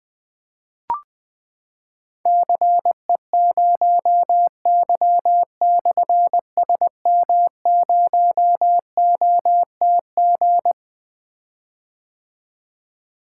Morse code, heard as a keyed tone, about 21 wpm, 700 Hz. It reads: CE0Y/SM0OTG
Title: Difficult Call Signs 20wpm